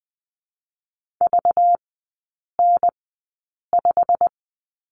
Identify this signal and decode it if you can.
Morse code, heard as a keyed tone, about 20 wpm, spaced as for 8 wpm, 690 Hz. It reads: VN5
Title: 191 Lesson 31e - Letter v - Three Characters 20wpm